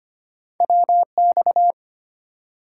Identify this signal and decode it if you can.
Morse code, heard as a keyed tone, about 25 wpm, 700 Hz. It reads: WX
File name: Top 90 QSO Elements - 3x Character Spacing - 25wpm